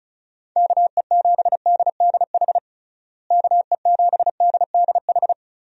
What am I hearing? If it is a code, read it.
Morse code, heard as a keyed tone, about 34 wpm, 700 Hz. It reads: KE7DDH KE7DDH
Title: USA Call Signs - 6 Characters - Repeated 3x - 35wpm